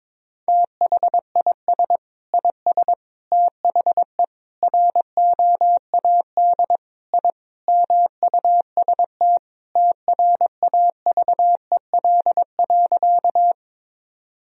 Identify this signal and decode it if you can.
Morse code, heard as a keyed tone, about 21 wpm, 700 Hz. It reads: THIS IS THE ROAD I MUST TRAVEL.
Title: Sentences from Top 1000 Words - Review Part 2 - 22wpm